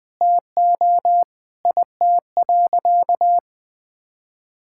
Morse code, heard as a keyed tone, about 20 wpm, 700 Hz. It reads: TO IT.
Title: Sentences from Top 1000 Words - Review Part 5 - 20wpm